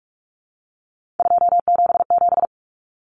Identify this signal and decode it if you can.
Morse code, heard as a keyed tone, about 44 wpm, 700 Hz. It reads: 277
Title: Common Cut Numbers - 3-Digit 45wpm